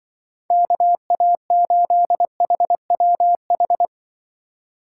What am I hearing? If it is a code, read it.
Morse code, heard as a keyed tone, about 23 wpm, 700 Hz. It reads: KA8HWH
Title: Speed Racing - USA 2x3 Call Signs 20wpm